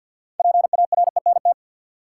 Morse code, heard as a keyed tone, about 49 wpm, 700 Hz. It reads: PARENT
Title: Six Letter Words - 50wpm